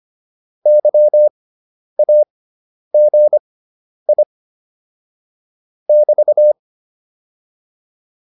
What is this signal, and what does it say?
Morse code, heard as a keyed tone, about 25 wpm, spaced as for 10 wpm, 600 Hz. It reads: YAGI <BT>